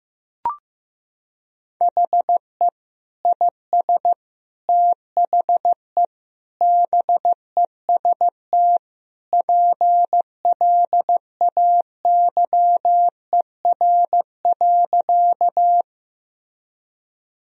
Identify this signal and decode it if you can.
Morse code, heard as a keyed tone, about 16 wpm, 700 Hz. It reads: HE IS THE BEST PLAYER.
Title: Sentences from Top 1000 Words - Review Part 4 - 15wpm